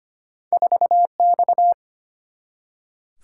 Morse code, heard as a keyed tone, about 25 wpm, 700 Hz. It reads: 4X
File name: Two Letter-Number Combinations - 25wpm